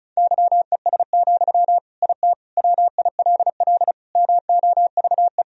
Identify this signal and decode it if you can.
Morse code, heard as a keyed tone, about 34 wpm, 700 Hz. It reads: YES, IT WILL MOVE
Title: Sentences from Top 1000 Words - Review Part 1 - 35wpm